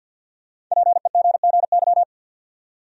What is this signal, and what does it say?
Morse code, heard as a keyed tone, about 52 wpm, 700 Hz. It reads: PEGGY
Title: USA Common Names - Max 5 Letters - 50wpm